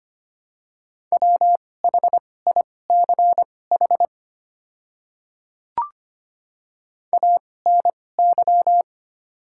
Morse code, heard as a keyed tone, about 25 wpm, spaced as for 18 wpm, 700 Hz. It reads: WHICH ANY
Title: 2x Character Spacing - Top 300 Words - 25wpm